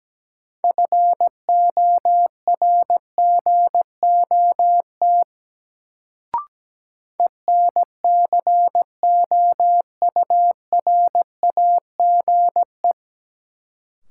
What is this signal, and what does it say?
Morse code, heard as a keyed tone, about 17 wpm, 700 Hz. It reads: FORGOT ENCOURAGE
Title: Sentences from Top 1400 Words - New Word Drill - 17wpm